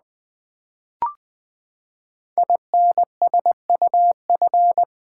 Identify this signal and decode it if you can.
Morse code, heard as a keyed tone, about 20 wpm, 700 Hz. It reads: INSUF
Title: Build A Word - 5 Letters At A Time - 12-Letter Words - 20wpm